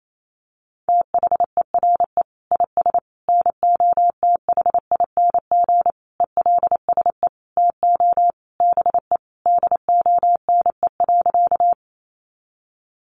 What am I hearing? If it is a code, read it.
Morse code, heard as a keyed tone, about 28 wpm, 700 Hz. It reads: THERE IS NOTHING ELSE TO BE DONE.
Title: Sentences from Top 800 Words 28wpm